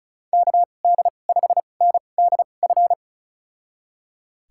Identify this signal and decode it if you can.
Morse code, heard as a keyed tone, about 36 wpm, spaced as for 25 wpm, 700 Hz. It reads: KD5NDF
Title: USA Call Signs - 2x Character Spacing - 35wpm